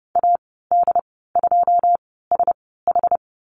Morse code, heard as a keyed tone, about 30 wpm, spaced as for 17 wpm, 700 Hz. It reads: AD2SH
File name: Lesson 14g - Number 2 - Call Signs 30wpm